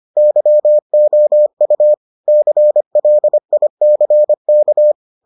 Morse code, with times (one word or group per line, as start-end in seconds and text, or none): 0.2-1.9 YOU
2.3-4.9 CLICK